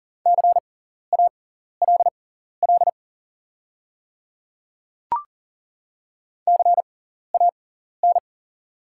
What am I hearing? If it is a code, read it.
Morse code, heard as a keyed tone, about 40 wpm, spaced as for 14 wpm, 700 Hz. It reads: CALL CAN
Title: 6x Character Spacing - Top 200 Words - 40wpm